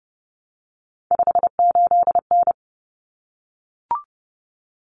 Morse code, heard as a keyed tone, about 30 wpm, 700 Hz. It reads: 58N
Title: Common Cut Numbers - 3-Digit 30wpm